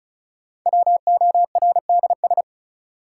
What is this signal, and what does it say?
Morse code, heard as a keyed tone, about 36 wpm, 700 Hz. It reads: WORDS
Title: Sentences from Top 1000 Words - Review Part 6 - New Word Drill - 35wpm